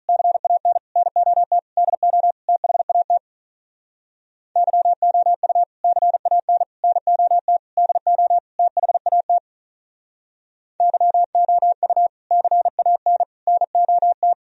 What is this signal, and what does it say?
Morse code, heard as a keyed tone, about 46 wpm, 700 Hz. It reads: CAN NOT DO THAT YOU CAN NOT DO THAT YOU CAN NOT